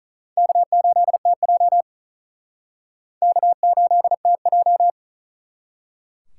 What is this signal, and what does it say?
Morse code, heard as a keyed tone, about 43 wpm, 700 Hz. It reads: K8TJ K8TJ
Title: Speed Racing - USA 1x2 Call Signs 35wpm